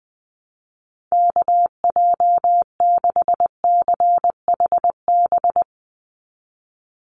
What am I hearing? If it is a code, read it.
Morse code, heard as a keyed tone, about 20 wpm, 700 Hz. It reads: KJ6CHB
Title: USA Call Signs 20wpm